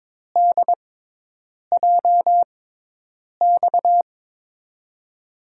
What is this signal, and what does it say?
Morse code, heard as a keyed tone, about 22 wpm, spaced as for 8 wpm, 700 Hz. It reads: DJX
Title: USA Call Signs - 5 Characters - 6x Character Spacing - 22wpm